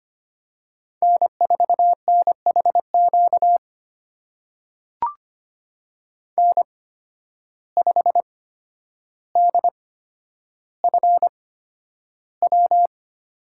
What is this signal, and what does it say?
Morse code, heard as a keyed tone, about 25 wpm, spaced as for 11 wpm, 700 Hz. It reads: N4NHQ N5DFW
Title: USA Call Signs - 8x Character Spacing - 25wpm